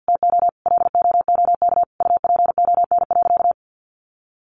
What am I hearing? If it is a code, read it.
Morse code, heard as a keyed tone, about 50 wpm, 700 Hz. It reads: TO LOOK UPON.
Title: Sentences from Top 1400 Words 50wpm